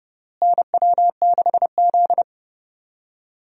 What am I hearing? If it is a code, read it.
Morse code, heard as a keyed tone, about 30 wpm, 700 Hz. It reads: NW6Z
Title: USA Call Signs - 4 Characters - Repeated 3x - 30wpm